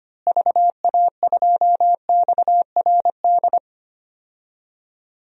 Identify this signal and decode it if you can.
Morse code, heard as a keyed tone, about 25 wpm, 700 Hz. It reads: VA2XRD
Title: DXCC-100 Call Signs - 6 Characters - Repeated 3x - 25wpm